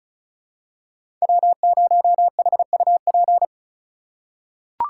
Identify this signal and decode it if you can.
Morse code, heard as a keyed tone, about 34 wpm, 700 Hz. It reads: W0HUP
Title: USA Call Signs - 5 Characters - 4x Character Spacing - 35wpm